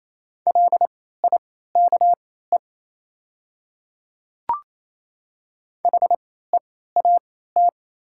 Morse code, heard as a keyed tone, about 27 wpm, spaced as for 16 wpm, 690 Hz. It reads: LIKE HEAT
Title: 3x Character Spacing - Top 500 Words - 28wpm